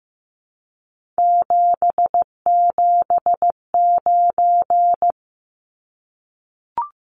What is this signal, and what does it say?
Morse code, heard as a keyed tone, about 15 wpm, 700 Hz. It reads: 779
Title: Numbers - 3-Digit 15wpm